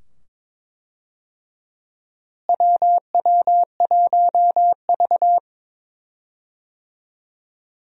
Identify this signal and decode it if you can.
Morse code, heard as a keyed tone, about 22 wpm, 700 Hz. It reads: WW1V
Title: USA Call Signs - 4 Characters - Repeated 2x - 22wpm